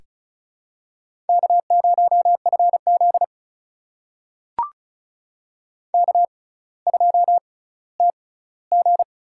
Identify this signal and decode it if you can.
Morse code, heard as a keyed tone, about 36 wpm, spaced as for 18 wpm, 700 Hz. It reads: K0FZ K2TG
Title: USA Call Signs - 4 Characters - 6x Character Spacing - 35wpm